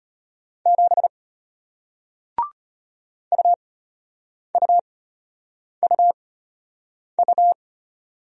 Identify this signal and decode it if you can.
Morse code, heard as a keyed tone, about 33 wpm, spaced as for 8 wpm, 700 Hz. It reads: 7 UUUU